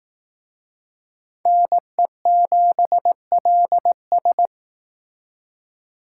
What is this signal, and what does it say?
Morse code, heard as a keyed tone, about 19 wpm, 700 Hz. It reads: NE7LS
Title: Speed Racing - USA 2x2 Call Signs 15wpm